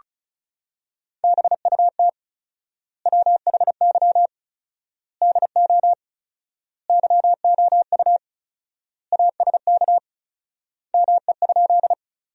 Morse code, heard as a keyed tone, about 34 wpm, 700 Hz. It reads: BUT WHY DO YOU ASK ME?
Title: 4x Word-Spacing - Sentences from Top 200 Words - 35wpm